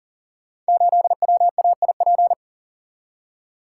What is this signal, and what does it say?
Morse code, heard as a keyed tone, about 40 wpm, 700 Hz. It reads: 8WAIP
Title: Code Groups - 5 Letter:Numbers - 40wpm